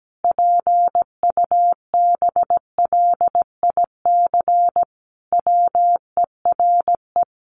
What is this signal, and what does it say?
Morse code, heard as a keyed tone, about 17 wpm, 700 Hz. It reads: PUBLIC WERE
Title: Sentences from Top 800 Words 17wpm